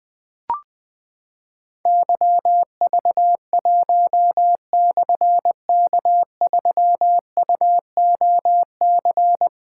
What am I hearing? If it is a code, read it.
Morse code, heard as a keyed tone, about 20 wpm, 700 Hz. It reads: YV1/K3UOC
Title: Difficult Call Signs 20wpm